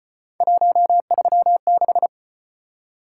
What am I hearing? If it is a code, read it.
Morse code, heard as a keyed tone, about 33 wpm, 700 Hz. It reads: 136